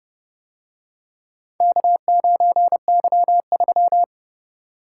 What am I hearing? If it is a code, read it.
Morse code, heard as a keyed tone, about 29 wpm, 700 Hz. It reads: K9Y3